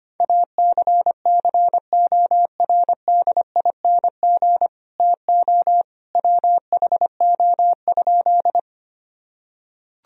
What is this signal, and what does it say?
Morse code, heard as a keyed tone, about 25 wpm, 700 Hz. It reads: ACCORDING TO WHO?
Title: Sentences from Top 1000 Words - Review Part 4 - 25wpm